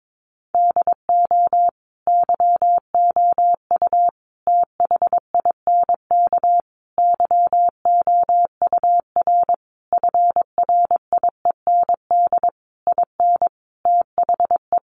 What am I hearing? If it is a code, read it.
Morse code, heard as a keyed tone, about 22 wpm, 700 Hz. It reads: DO YOU THINK YOUR FRIEND IN THE
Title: Sentences from Top 1100 Words 22wpm